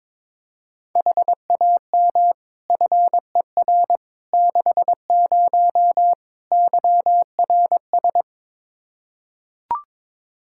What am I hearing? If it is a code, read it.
Morse code, heard as a keyed tone, about 21 wpm, 700 Hz. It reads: HAM FER 60 YRS
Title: Practice QSO - Exchange 3 - More Info - 22wpm